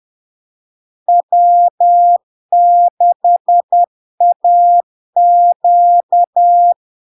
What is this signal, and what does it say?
Morse code, heard as a keyed tone, about 10 wpm, 700 Hz. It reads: W6AQ
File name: LARK-FD-2022-Log-in-CW-10wpm-700-Hz